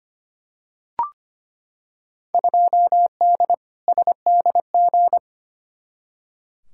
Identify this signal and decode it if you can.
Morse code, heard as a keyed tone, about 25 wpm, 700 Hz. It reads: 2D SDG
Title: Field Day Exchanges Encore - 25wpm